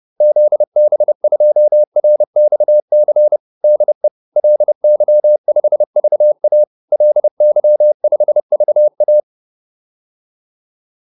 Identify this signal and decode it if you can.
Morse code, heard as a keyed tone, about 30 wpm, 600 Hz. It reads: ZB2RXC DE LY5VA LY5VA
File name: QSO406 30